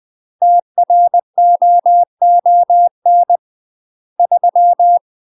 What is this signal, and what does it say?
Morse code, heard as a keyed tone, about 20 wpm, 700 Hz. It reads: TROON 3
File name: UK MASTER QSOs TRAINING / 20WPM / UK MASTER QSOs 0063